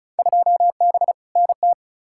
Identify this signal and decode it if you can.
Morse code, heard as a keyed tone, about 34 wpm, 700 Hz. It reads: 2B NT
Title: Field Day Exchanges Encore - 35wpm